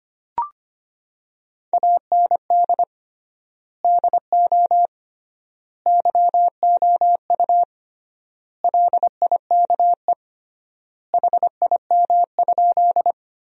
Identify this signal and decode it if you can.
Morse code, heard as a keyed tone, about 25 wpm, 700 Hz. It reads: AND DO YOU LIKE HIM?
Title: 3x Word-Spacing - Sentences from Top 100 Words - 25wpm